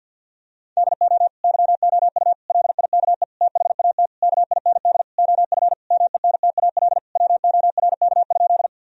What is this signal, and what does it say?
Morse code, heard as a keyed tone, about 50 wpm, 700 Hz. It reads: DO YOU LIKE THAT KIND OF MENTAL WORK?
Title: Sentences from Top 1000 Words - Review Part 4 - 50wpm